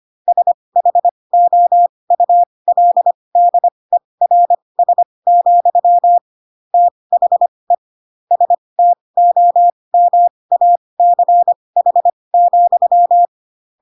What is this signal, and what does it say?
Morse code, heard as a keyed tone, about 25 wpm, spaced as for 20 wpm, 700 Hz. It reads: SHOULDERS, THE STOMACH,
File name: Pinocchio / Pinocchio ch3 15 25 ik1hge